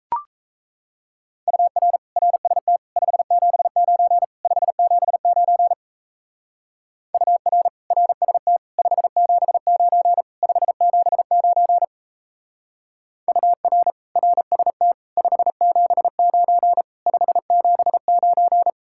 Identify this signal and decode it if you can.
Morse code, heard as a keyed tone, about 37 wpm, 700 Hz. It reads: UR RST 579 579 UR RST 579 579 UR RST 579 579